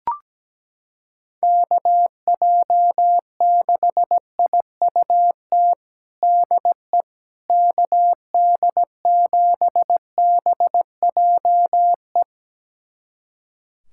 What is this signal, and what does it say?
Morse code, heard as a keyed tone, about 17 wpm, 700 Hz. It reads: KJ6IUT DE KD7BJE
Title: USA Callsign DE Callsign 17wpm